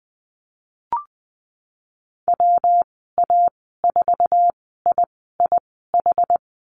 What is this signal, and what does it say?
Morse code, heard as a keyed tone, about 19 wpm, spaced as for 14 wpm, 700 Hz. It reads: WA4IIH
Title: 154 Lesson 25g - Letter p - Call Signs 20wpm